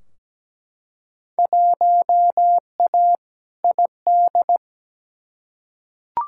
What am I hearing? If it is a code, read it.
Morse code, heard as a keyed tone, about 17 wpm, 700 Hz. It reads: 1A ID
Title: Field Day Exchanges - 17wpm